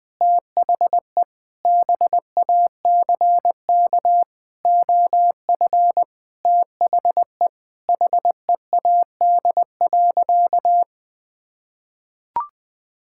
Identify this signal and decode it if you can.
Morse code, heard as a keyed tone, about 20 wpm, 700 Hz. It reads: THE BACK OF THE HEAD.
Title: Sentences with 6-Word Phrases - 20wpm